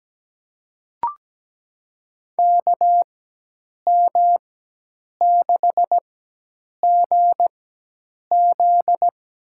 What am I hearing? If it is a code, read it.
Morse code, heard as a keyed tone, about 17 wpm, spaced as for 8 wpm, 700 Hz. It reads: KM6GZ
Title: USA Call Signs - 4x Character Spacing - 17wpm